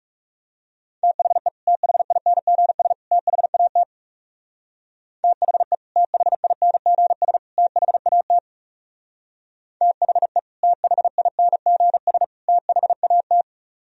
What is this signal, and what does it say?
Morse code, heard as a keyed tone, about 35 wpm, 700 Hz. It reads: THE THINGS THAT THE THINGS THAT THE THINGS THAT
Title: Speed-Racing - Sets of 3 Words 30wpm